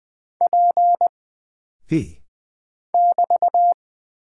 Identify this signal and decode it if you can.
Morse code, heard as a keyed tone, about 20 wpm, 700 Hz. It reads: P <BT>
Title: Single Character (Letter:Number:Punctuation:Prosign) - Mind-Melt - 20wpm